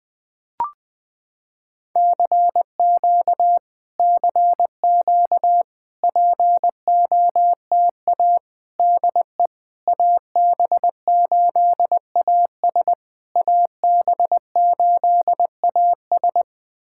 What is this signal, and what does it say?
Morse code, heard as a keyed tone, about 20 wpm, 700 Hz. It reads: CQ CQ POTA DE AB8AS AB8AS
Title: POTA QSO Practice - 20wpm